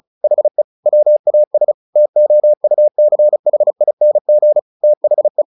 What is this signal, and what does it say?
Morse code, heard as a keyed tone, about 34 wpm, 600 Hz. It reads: HE WAS TOUCHING THE
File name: the old man and the sea / the old man and the sea 1 35wpm